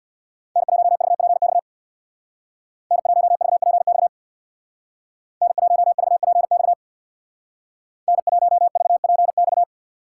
Morse code, heard as a keyed tone, about 64 wpm, 700 Hz. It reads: N1VPX N1VPX N1VPX N1VPX